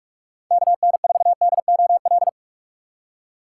Morse code, heard as a keyed tone, about 46 wpm, 700 Hz. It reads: KN4DOL
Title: USA Call Signs - 6 Characters - 6x Character Spacing - 45wpm